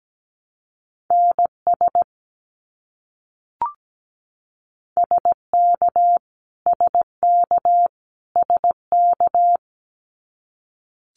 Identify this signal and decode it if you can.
Morse code, heard as a keyed tone, about 17 wpm, 700 Hz. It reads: NS SK SK SK
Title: Canadian Provinces and Territories - Repeated 3x - 17wpm